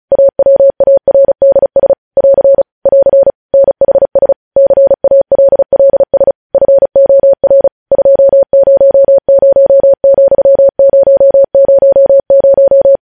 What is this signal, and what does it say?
Morse code, heard as a keyed tone, about 36 wpm, 550 Hz. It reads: AWARDS <AR> <AR> NHS CALLS FOR 200,000
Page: news headlines in morse code at 35 wpm